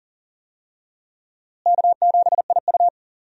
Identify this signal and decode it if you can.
Morse code, heard as a keyed tone, about 37 wpm, 700 Hz. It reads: K7IU
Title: USA Call Signs - 4 Characters - Repeated 2x - 40wpm